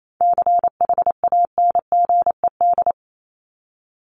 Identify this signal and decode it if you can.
Morse code, heard as a keyed tone, about 28 wpm, 700 Hz. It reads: CHANGED
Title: ED Suffix - Top 100 Words with Base Words 28wpm